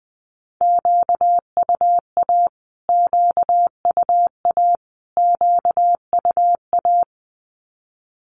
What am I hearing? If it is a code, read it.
Morse code, heard as a keyed tone, about 20 wpm, 700 Hz. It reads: QUA QUA QUA
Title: Q-Codes Repeated 3x 20wpm